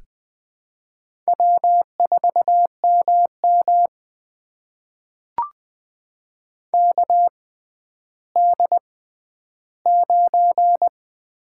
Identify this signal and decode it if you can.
Morse code, heard as a keyed tone, about 21 wpm, spaced as for 12 wpm, 700 Hz. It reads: W4MM KD9
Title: USA Call Signs - 4 Characters - 6x Character Spacing - 20wpm